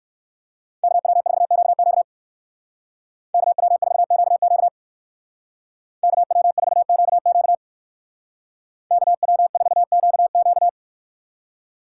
Morse code, heard as a keyed tone, about 58 wpm, 700 Hz. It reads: KW4QQ KW4QQ KW4QQ KW4QQ